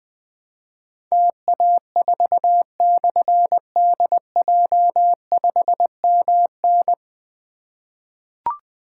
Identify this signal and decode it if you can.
Morse code, heard as a keyed tone, about 20 wpm, 700 Hz. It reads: TA4/DJ5MN
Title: Difficult Call Signs 20wpm